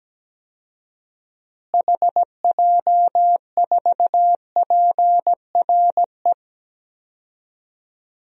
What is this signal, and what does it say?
Morse code, heard as a keyed tone, about 17 wpm, 700 Hz. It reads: HJ4PRE
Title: DXCC-100 Call Signs - 6 Characters - Repeated 2x - 17wpm